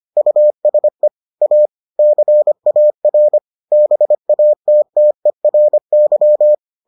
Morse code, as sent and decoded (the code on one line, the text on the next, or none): ..- ... / .- / -.-. .- .-. / -... .- - - . .-. -.--
USE A CAR BATTERY